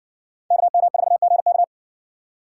Morse code, heard as a keyed tone, about 60 wpm, 700 Hz. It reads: KM4GK